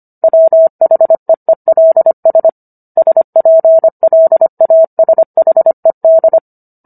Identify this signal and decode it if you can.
Morse code, heard as a keyed tone, about 25 wpm, 650 Hz. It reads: WHEELS SPLASHED